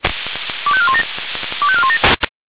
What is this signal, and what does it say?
Five-tone selective call (ZVEI), heard as quick runs of five tones, tones of about 70 ms. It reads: 25417 25417